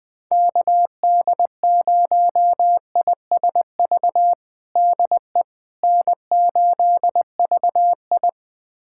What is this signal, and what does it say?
Morse code, heard as a keyed tone, about 20 wpm, 700 Hz. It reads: KD0ISV DE N8VI
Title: USA Callsign DE Callsign 20wpm